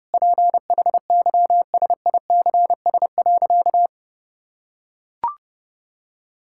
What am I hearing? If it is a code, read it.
Morse code, heard as a keyed tone, about 30 wpm, 700 Hz. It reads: PHYSICS.
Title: Sentences from Top 1000 Words - Review Part 5 - 30wpm